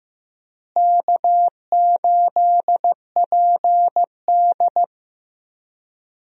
Morse code, heard as a keyed tone, about 15 wpm, 700 Hz. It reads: K8PD
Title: USA Call Signs - 4 Characters - Repeated 3x - 15wpm